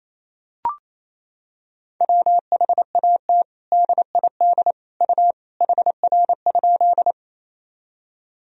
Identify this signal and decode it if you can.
Morse code, heard as a keyed tone, about 28 wpm, 700 Hz. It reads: WHAT DID U HR?
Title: Sentences from Top 300 Words and Abbreviations 28wpm